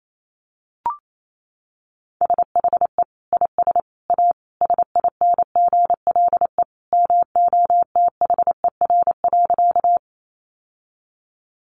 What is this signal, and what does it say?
Morse code, heard as a keyed tone, about 28 wpm, 700 Hz. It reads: SHE IS A SINGLE MOTHER.
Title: Sentences from Top 700 Words 28wpm